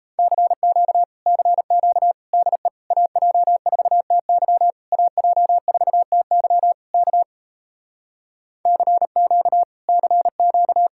Morse code, heard as a keyed tone, about 38 wpm, 700 Hz. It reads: CQ CQ DE AJ4TY AJ4TY K CQ CQ